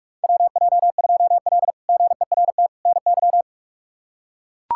Morse code, heard as a keyed tone, about 42 wpm, 700 Hz. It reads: WJ2L GERT NY
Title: K1USN SST QSO Practice - 45wpm